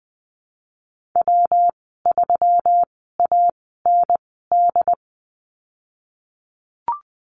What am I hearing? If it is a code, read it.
Morse code, heard as a keyed tone, about 20 wpm, spaced as for 14 wpm, 700 Hz. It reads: W3AND
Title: 140 Lesson 23g - Letter f - Call Signs 20wpm